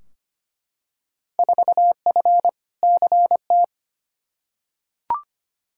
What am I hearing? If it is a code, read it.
Morse code, heard as a keyed tone, about 25 wpm, 700 Hz. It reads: 4F CT
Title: Field Day Exchanges Encore - 25wpm